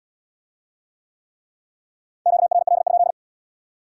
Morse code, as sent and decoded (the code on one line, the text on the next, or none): --.- .-. .-.. ..--..
QRL?